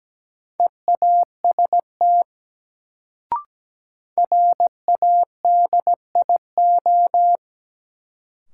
Morse code, heard as a keyed tone, about 17 wpm, 700 Hz. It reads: EAST RADIO
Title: Sentences from Top 1000 Words - Review Part 3 - New Word Drill - 17wpm